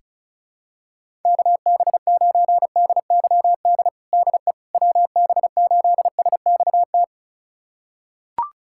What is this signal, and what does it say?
Morse code, heard as a keyed tone, about 36 wpm, 700 Hz. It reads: KB9DYD DE WB8SXT
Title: USA Callsign DE Callsign - 6 Characters - 35wpm